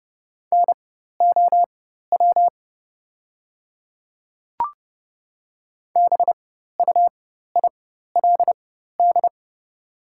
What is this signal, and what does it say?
Morse code, heard as a keyed tone, about 30 wpm, spaced as for 14 wpm, 700 Hz. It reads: NOW BUILD
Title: 4x Character Spacing - Top 200 Words - 30wpm